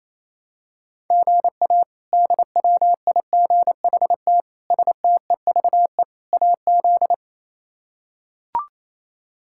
Morse code, heard as a keyed tone, about 28 wpm, 700 Hz. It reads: GA DWIGHT STEVE AZ
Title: K1USN SST QSO Practice - 28wpm